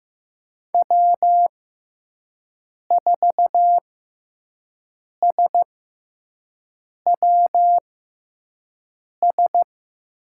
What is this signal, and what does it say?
Morse code, heard as a keyed tone, about 15 wpm, spaced as for 5 wpm, 700 Hz. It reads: W4SWS
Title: USA Call Signs - 5 Characters - 6x Character Spacing - 15wpm